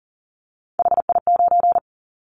Morse code, heard as a keyed tone, about 40 wpm, 700 Hz. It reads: HI9
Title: Callsign Prefixes for Top 100 DXCC - 40wpm